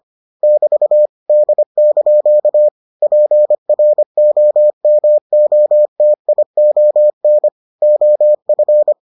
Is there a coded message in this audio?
Yes: Morse code from 0.4 to 8.9 s, heard as a keyed tone, about 25 wpm, 600 Hz. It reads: <BT> D) PROMOTION OF